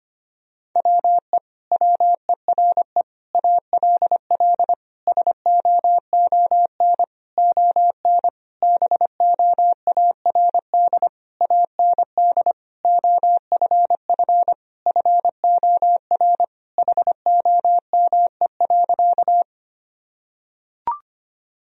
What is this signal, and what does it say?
Morse code, heard as a keyed tone, about 25 wpm, 700 Hz. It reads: WE WERE ALL SOON ON BOARD AND OFF FOR HOME.
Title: Sentences from Top 1000 Words - Review Part 4 - 25wpm